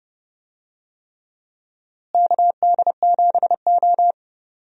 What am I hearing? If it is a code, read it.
Morse code, heard as a keyed tone, about 30 wpm, 700 Hz. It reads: KD7O
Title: USA Call Signs - 4 Characters - Repeated 3x - 30wpm